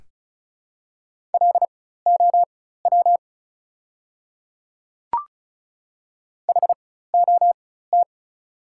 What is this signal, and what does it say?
Morse code, heard as a keyed tone, about 34 wpm, spaced as for 16 wpm, 690 Hz. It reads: LOW HOT
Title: 4x Character Spacing - Top 500 Words - 35wpm